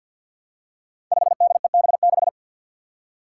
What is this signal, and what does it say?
Morse code, heard as a keyed tone, about 47 wpm, 700 Hz. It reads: 5DEB6
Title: Five Challenging Letter-Number Combinations 50wpm